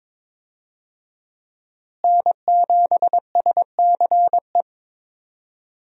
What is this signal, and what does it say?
Morse code, heard as a keyed tone, about 22 wpm, 700 Hz. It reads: N7SCE
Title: USA Call Signs - 5 Characters - Repeated 3x - 22wpm